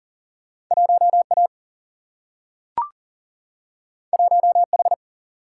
Morse code, heard as a keyed tone, about 40 wpm, 700 Hz. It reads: 1A 1H
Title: Single-Letter Single-Number 40wpm